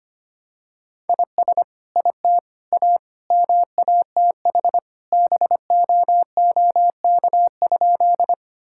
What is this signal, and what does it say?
Morse code, heard as a keyed tone, about 25 wpm, 700 Hz. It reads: IS IT A MATH BOOK?
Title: Sentences from Top 1000 Words - Review Part 6 - 25wpm